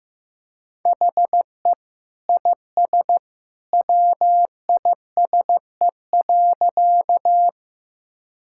Morse code, heard as a keyed tone, about 15 wpm, 700 Hz. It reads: HE IS WISE.
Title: Sentences from Top 1600 Words 15wpm